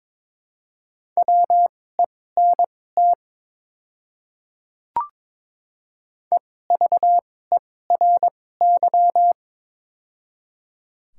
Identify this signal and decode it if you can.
Morse code, heard as a keyed tone, about 22 wpm, spaced as for 16 wpm, 700 Hz. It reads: WENT EVERY
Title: 2x Character Spacing - Top 300 Words - 22wpm